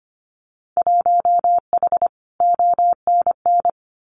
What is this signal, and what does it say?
Morse code, heard as a keyed tone, about 25 wpm, 700 Hz. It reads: 1H ONN
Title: Winter Field Day Exchanges 25wpm